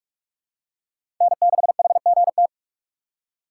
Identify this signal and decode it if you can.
Morse code, heard as a keyed tone, about 46 wpm, 690 Hz. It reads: N6HGT
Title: USA Call Signs - 5 Characters - 2x Character Spacing - 45wpm